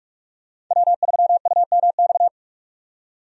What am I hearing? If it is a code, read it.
Morse code, heard as a keyed tone, about 46 wpm, 700 Hz. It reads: W3UMX
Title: USA Call Signs - 5 Characters - 45wpm